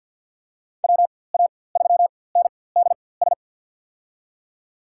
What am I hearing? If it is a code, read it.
Morse code, heard as a keyed tone, about 53 wpm, spaced as for 24 wpm, 700 Hz. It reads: WA3NDS
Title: USA Call Signs - 6 Characters - 4x Character Spacing - 50wpm